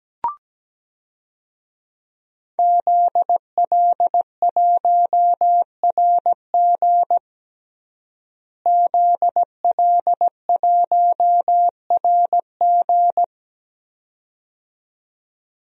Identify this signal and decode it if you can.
Morse code, heard as a keyed tone, about 17 wpm, 700 Hz. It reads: ZL1RG ZL1RG
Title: DXCC-100 Call Signs - 5 Characters - Repeated 2x - 17wpm